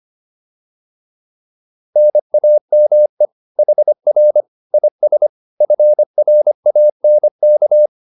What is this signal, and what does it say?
Morse code, heard as a keyed tone, about 25 wpm, 600 Hz. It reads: NAME HR IS FRANK
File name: QSO208 25